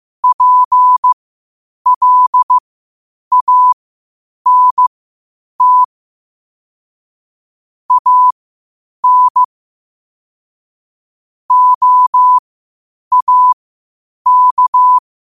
Morse code, heard as a keyed tone, about 15 wpm, spaced as for 8 wpm, 1000 Hz. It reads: PLANT AN OAK
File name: qotdfiles / 8wpm